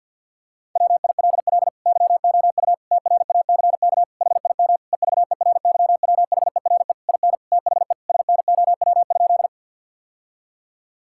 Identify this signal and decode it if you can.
Morse code, heard as a keyed tone, about 50 wpm, 700 Hz. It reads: WILL YOU TRACK HIM EVERYWHERE IN THE SNOW?
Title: Sentences from Top 1600 Words 50wpm